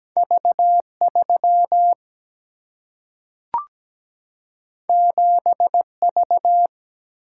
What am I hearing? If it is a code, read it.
Morse code, heard as a keyed tone, about 17 wpm, 700 Hz. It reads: V3 7V
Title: Single-Letter Single-Number 17wpm